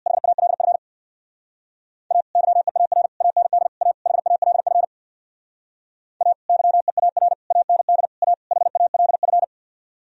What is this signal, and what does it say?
Morse code, heard as a keyed tone, about 62 wpm, 700 Hz. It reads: HALF A YEAR AND A HALF A YEAR AND A HALF
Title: Speed-Racing - Sets of 5 Words 50wpm